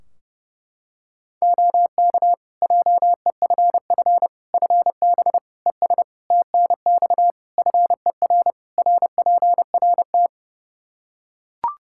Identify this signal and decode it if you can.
Morse code, heard as a keyed tone, about 30 wpm, 700 Hz. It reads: OK JEFF FB ES TNX FER RPRT
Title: K1USN SST QSO Practice 30wpm